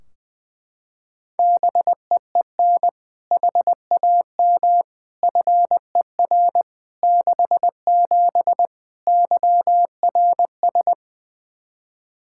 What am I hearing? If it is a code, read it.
Morse code, heard as a keyed tone, about 21 wpm, 700 Hz. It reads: BEEN HAM FER 67 YRS
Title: Practice QSO - Exchange 3 - More Info - 20wpm